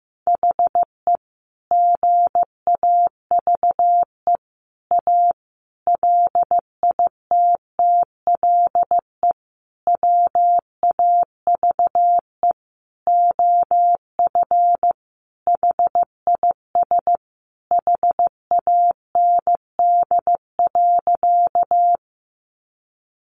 Morse code, heard as a keyed tone, about 15 wpm, 700 Hz. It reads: HE GAVE A LITTLE WAVE OF HIS HAND.
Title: Sentences from Top 1400 Words 15wpm